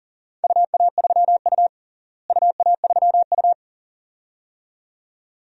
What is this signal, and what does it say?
Morse code, heard as a keyed tone, about 40 wpm, 700 Hz. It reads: UA3U UA3U
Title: DXCC-100 Call Signs - 4 Characters - Repeated 2x - 40wpm